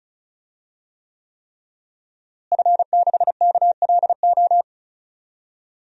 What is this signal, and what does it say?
Morse code, heard as a keyed tone, about 36 wpm, 700 Hz. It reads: F6KLO
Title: DXCC-100 Call Signs - 5 Characters - Repeated 3x - 35wpm